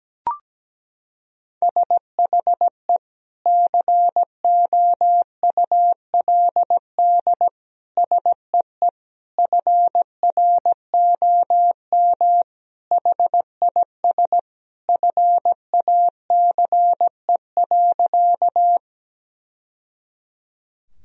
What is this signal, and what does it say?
Morse code, heard as a keyed tone, about 17 wpm, 700 Hz. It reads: SHE COULD SEE FROM HIS FACE.
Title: Sentences from Top 1000 Words - Review Part 2 - 17wpm